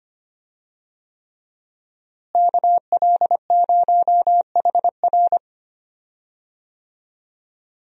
Morse code, heard as a keyed tone, about 25 wpm, 700 Hz. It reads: KL0HR
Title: USA Call Signs - 5 Characters - Repeated 3x - 25wpm